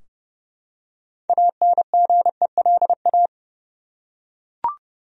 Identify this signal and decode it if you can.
Morse code, heard as a keyed tone, about 30 wpm, 700 Hz. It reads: ANGELA
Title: USA Common Names - 30wpm